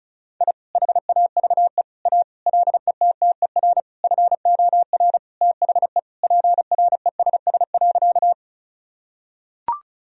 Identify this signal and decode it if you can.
Morse code, heard as a keyed tone, about 36 wpm, 700 Hz. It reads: I HAVE A LETTER FOR THE PRESS.
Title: Sentences from Top 1000 Words - Review Part 1 - 35wpm